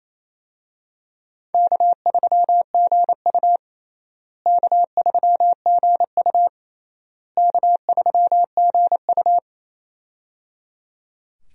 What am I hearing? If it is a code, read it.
Morse code, heard as a keyed tone, about 27 wpm, 700 Hz. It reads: K3GU K3GU K3GU
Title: USA Call Signs - 4 Characters - Repeated 3x - 28wpm